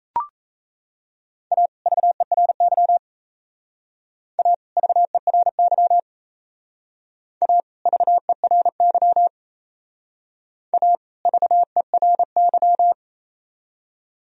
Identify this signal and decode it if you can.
Morse code, heard as a keyed tone, about 35 wpm, 700 Hz. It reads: A VERY A VERY A VERY A VERY
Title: Speed-Racing - Sets of 2 Words 28wpm